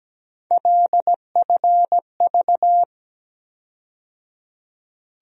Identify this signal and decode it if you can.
Morse code, heard as a keyed tone, about 17 wpm, 700 Hz. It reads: LFV
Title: USA Call Sign Postfixes - 3 Letters 17wpm